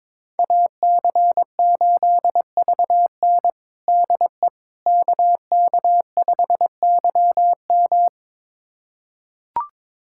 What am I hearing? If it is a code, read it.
Morse code, heard as a keyed tone, about 21 wpm, 700 Hz. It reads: AC8VN DE KK5YM
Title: USA Callsign DE Callsign - 5 Characters - 22wpm